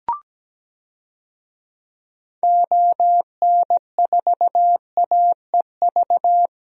Morse code, heard as a keyed tone, about 17 wpm, 700 Hz. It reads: ON4AEV
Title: DXCC-100 Call Signs - 6 Characters - Repeated 3x - 17wpm